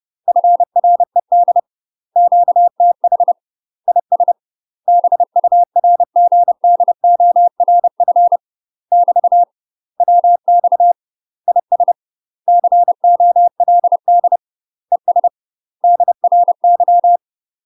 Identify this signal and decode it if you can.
Morse code, heard as a keyed tone, about 30 wpm, 700 Hz. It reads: FRED QTH IS BURGDORF <BT> WX IS COLD ES DRY
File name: UK MASTER QSOs TRAINING / 30WPM / UK MASTER QSOs 0007